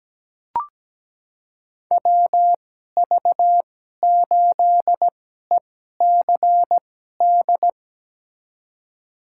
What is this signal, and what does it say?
Morse code, heard as a keyed tone, about 17 wpm, spaced as for 12 wpm, 700 Hz. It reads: WV8ECD
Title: USA Call Signs - 2x Character Spacing - 17wpm